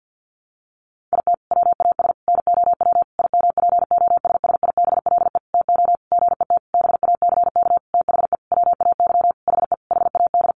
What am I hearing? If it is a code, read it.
Morse code, heard as a keyed tone, about 50 wpm, 700 Hz. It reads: IT WAS NOW IMPOSSIBLE TO GET BACK THE WAY HE HAD
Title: Sentences from Top 1100 Words 50wpm